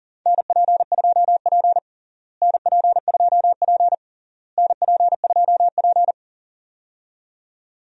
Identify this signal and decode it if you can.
Morse code, heard as a keyed tone, about 40 wpm, 700 Hz. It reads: NP2P NP2P NP2P
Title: USA Call Signs - 4 Characters - Repeated 3x - 40wpm